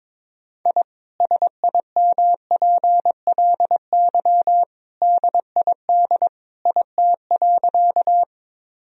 Morse code, heard as a keyed tone, about 22 wpm, 700 Hz. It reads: I SIMPLY DID IT.
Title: Sentences from Top 1000 Words - Review Part 4 - 22wpm